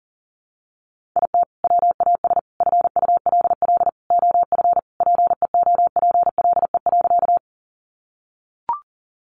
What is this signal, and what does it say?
Morse code, heard as a keyed tone, about 40 wpm, 700 Hz. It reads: IT WAS FULL OF PEOPLE.
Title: Sentences from Top 500 Words 40wpm